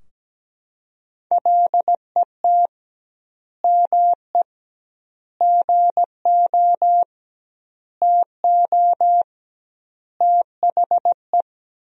Morse code, heard as a keyed tone, about 17 wpm, 700 Hz. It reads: LET ME GO TO THE
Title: 2x Word-Spacing - Sentences from Top 300 Words - 17wpm